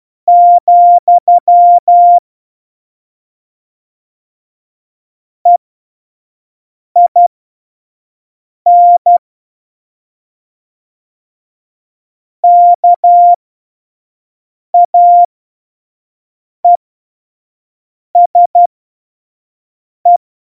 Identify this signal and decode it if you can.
Morse code, heard as a keyed tone, about 12 wpm, spaced as for 5 wpm, 700 Hz. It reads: , EIN KAESE
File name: Morsekurs / Witze / Witz 03 April 2018, 25 F70